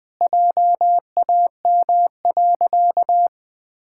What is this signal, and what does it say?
Morse code, heard as a keyed tone, about 20 wpm, 700 Hz. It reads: JAM.
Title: Pangram Challenge v2 - 20wpm